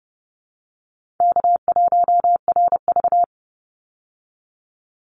Morse code, heard as a keyed tone, about 29 wpm, 700 Hz. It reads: K1RV
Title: Easy Call Signs Repeated 2x - 30wpm